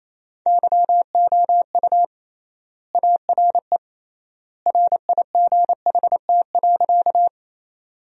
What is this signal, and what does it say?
Morse code, heard as a keyed tone, about 28 wpm, 700 Hz. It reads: YOU ARE RIGHT.
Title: 3x Word-Spacing - Sentences from Top 200 Words - 28wpm